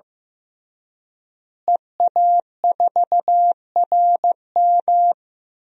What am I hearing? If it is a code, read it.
Morse code, heard as a keyed tone, about 15 wpm, 700 Hz. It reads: EA4RM
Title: DXCC-100 Call Signs - 5 Characters - Repeated 3x - 15wpm